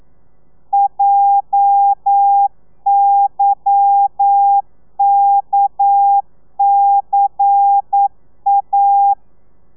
Morse code, heard as a keyed tone, about 9 wpm, 800 Hz. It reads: JYKCA